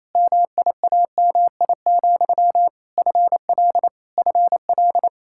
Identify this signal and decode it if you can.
Morse code, heard as a keyed tone, about 28 wpm, 700 Hz. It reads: MIAMI, FL FL
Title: K1USN SST QSO Practice 28wpm